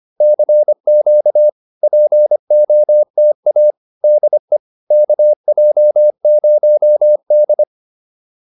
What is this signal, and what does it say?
Morse code, heard as a keyed tone, about 25 wpm, 600 Hz. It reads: CQ POTA DE KJ0D
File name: POTA102 25